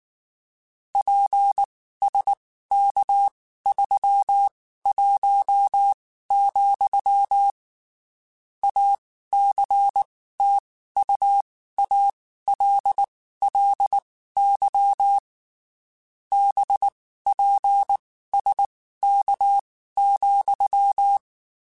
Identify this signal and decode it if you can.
Morse code, heard as a keyed tone, about 19 wpm, spaced as for 14 wpm, 800 Hz. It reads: PSK31, ACTUALLY BPSK,
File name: (spaced) / QSOs Spaced-mp3s / kc7haa @18 sp